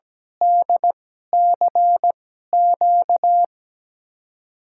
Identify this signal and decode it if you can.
Morse code, heard as a keyed tone, about 17 wpm, spaced as for 12 wpm, 700 Hz. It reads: DCQ
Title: USA Call Signs - 5 Characters - 2x Character Spacing - 17wpm